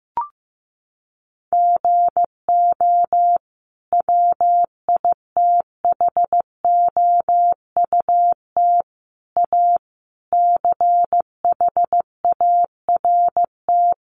Morse code, heard as a keyed tone, about 15 wpm, 700 Hz. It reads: GO WITHOUT A CHART
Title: Sentences from Top 1300 Words 15wpm